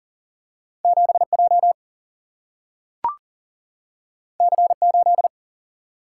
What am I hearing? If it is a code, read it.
Morse code, heard as a keyed tone, about 39 wpm, 700 Hz. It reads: 7 J C 8
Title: Single-Letter Single-Number 40wpm